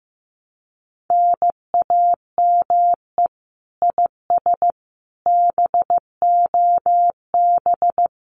Morse code, heard as a keyed tone, about 15 wpm, 700 Hz. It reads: NAME IS BOB
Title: Common QSO Phrases 15wpm